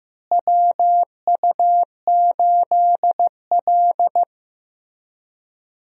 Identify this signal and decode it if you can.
Morse code, heard as a keyed tone, about 15 wpm, 700 Hz. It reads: WU8L
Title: USA Call Signs - 4 Characters - Repeated 3x - 15wpm